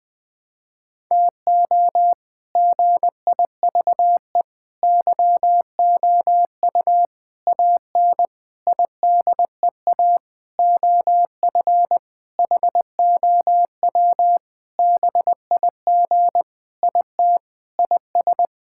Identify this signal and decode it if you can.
Morse code, heard as a keyed tone, about 21 wpm, 700 Hz. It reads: TO GIVE YOU AN IDEA OF HOW BIG IT IS
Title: Sentences with 7-Word Phrases - Spoken Beginning - 20wpm